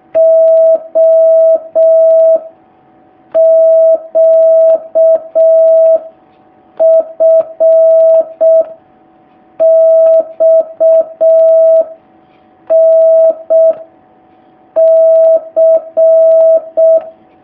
Morse code, heard as a keyed tone, about 6 wpm, 650 Hz. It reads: O Q F X N C
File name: jh8cbh / 2019cw